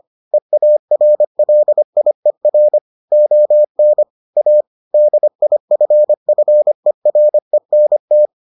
Morse code, heard as a keyed tone, about 25 wpm, 600 Hz. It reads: EARLIER ON A DIFFERENT